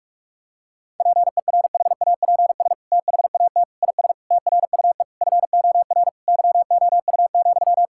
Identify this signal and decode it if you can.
Morse code, heard as a keyed tone, about 42 wpm, 700 Hz. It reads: PERHAPS THAT IS TRUE FOR YOU,
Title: Sentences from Top 1000 Words - Review Part 5 - 45wpm